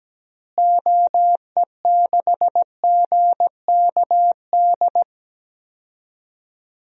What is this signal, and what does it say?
Morse code, heard as a keyed tone, about 17 wpm, 700 Hz. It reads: OE6GKD
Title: DXCC-100 Call Signs - 6 Characters - Repeated 2x - 17wpm